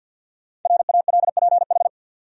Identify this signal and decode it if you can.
Morse code, heard as a keyed tone, about 52 wpm, 700 Hz. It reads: RALPH
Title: USA Common Names - Max 5 Letters - 50wpm